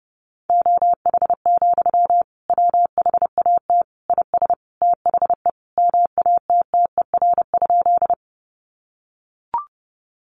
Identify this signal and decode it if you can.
Morse code, heard as a keyed tone, about 30 wpm, 700 Hz. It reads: OH, WHAT IS THE MATTER?
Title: Sentences from Top 700 Words 30wpm